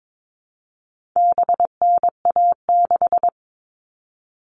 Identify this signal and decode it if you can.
Morse code, heard as a keyed tone, about 22 wpm, 700 Hz. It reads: BNA6
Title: All Cut Numbers - 4-Digit 22wpm